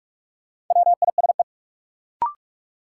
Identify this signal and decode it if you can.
Morse code, heard as a keyed tone, about 45 wpm, 700 Hz. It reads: WISE
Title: Sentences from Top 1600 Words - New Word Drill Alt - 45wpm